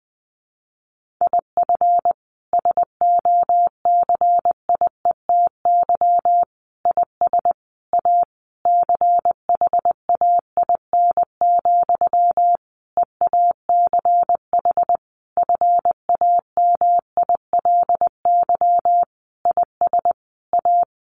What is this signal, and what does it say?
Morse code, heard as a keyed tone, about 19 wpm, 700 Hz. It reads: IF SOCIETY IS A CHAIN, EACH FAMILY IS A
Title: Sentences from Top 1500 Words 20wpm